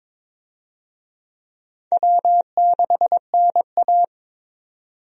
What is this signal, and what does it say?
Morse code, heard as a keyed tone, about 22 wpm, 700 Hz. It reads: W6NA
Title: USA Call Signs - 4 Characters - 22wpm